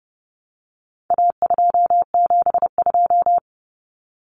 Morse code, heard as a keyed tone, about 30 wpm, 700 Hz. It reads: A272
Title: Common Cut Numbers - 4-Digit 30wpm